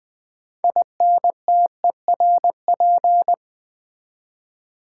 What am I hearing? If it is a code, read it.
Morse code, heard as a keyed tone, about 20 wpm, 700 Hz. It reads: INTERP
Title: Build A Word - 3 Letters At A Time - 9-Letter Words - 20wpm